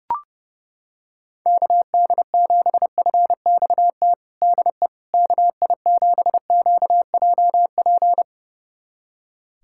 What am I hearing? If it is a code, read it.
Morse code, heard as a keyed tone, about 30 wpm, 700 Hz. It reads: KD7FXT DE KI7QJP
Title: USA Callsign DE Callsign - 6 Characters - 30wpm